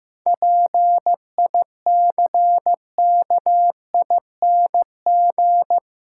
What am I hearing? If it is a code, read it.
Morse code, heard as a keyed tone, about 15 wpm, 700 Hz. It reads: PICKING
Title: Pangram Challenge v2 - 15wpm